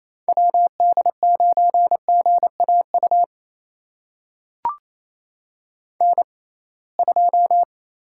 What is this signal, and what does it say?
Morse code, heard as a keyed tone, about 28 wpm, spaced as for 21 wpm, 700 Hz. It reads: WD9GAU N2
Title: USA Call Signs - 6x Character Spacing - 28wpm